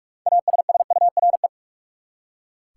Morse code, heard as a keyed tone, about 41 wpm, 700 Hz. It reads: ASSURE
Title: Six Letter Words - 45wpm